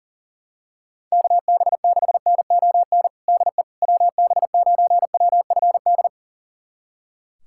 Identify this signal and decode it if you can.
Morse code, heard as a keyed tone, about 40 wpm, 700 Hz. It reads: KB6NON DE WB9WFD
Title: USA Callsign DE Callsign 40wpm